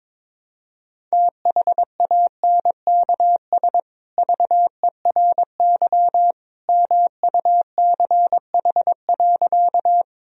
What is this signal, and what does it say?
Morse code, heard as a keyed tone, about 22 wpm, 700 Hz. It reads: THANKS VERY MUCH.
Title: Sentences from Top 1000 Words - Review Part 3 - 22wpm